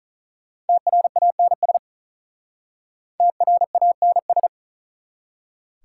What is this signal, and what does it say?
Morse code, heard as a keyed tone, about 44 wpm, 700 Hz. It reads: TRANS TRANS